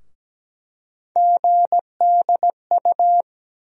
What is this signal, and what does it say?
Morse code, heard as a keyed tone, about 17 wpm, 700 Hz. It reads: GDU
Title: USA Call Sign Postfixes - 3 Letters 17wpm